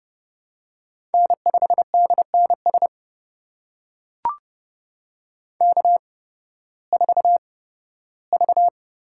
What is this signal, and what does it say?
Morse code, heard as a keyed tone, about 30 wpm, spaced as for 16 wpm, 700 Hz. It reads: N5DNS K4V
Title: USA Call Signs - 5 Characters - 8x Character Spacing - 30wpm